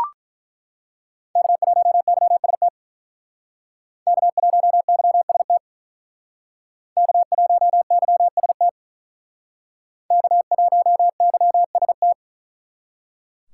Morse code, heard as a keyed tone, about 48 wpm, 700 Hz. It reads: K1YST K1YST K1YST K1YST